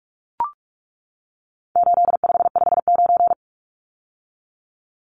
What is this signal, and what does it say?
Morse code, heard as a keyed tone, about 44 wpm, 700 Hz. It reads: 8559
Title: Numbers - 4-Digit 45wpm